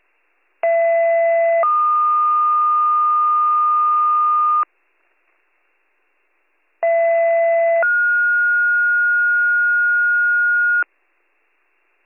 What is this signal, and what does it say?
Two-tone page, heard as a one-second tone, then a three-second tone, about 670 then 1220 Hz, about 670 then 1430 Hz. It is heard from 0.6 to 10.8 s.